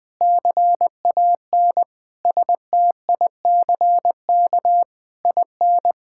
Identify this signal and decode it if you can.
Morse code, heard as a keyed tone, about 20 wpm, 700 Hz. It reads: CAN STICK IN
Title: Difficult Sentences with Hints - 20wpm